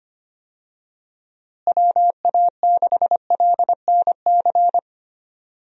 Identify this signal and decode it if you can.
Morse code, heard as a keyed tone, about 25 wpm, 700 Hz. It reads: WA6LNC
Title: USA Call Signs - 6 Characters - Repeated 3x - 25wpm